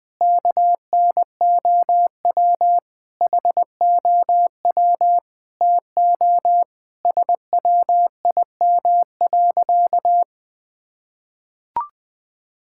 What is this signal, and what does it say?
Morse code, heard as a keyed tone, about 21 wpm, 700 Hz. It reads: KNOW HOW TO SWIM.
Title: Sentences with 4-Word Phrases - Spoken Beginning - 20wpm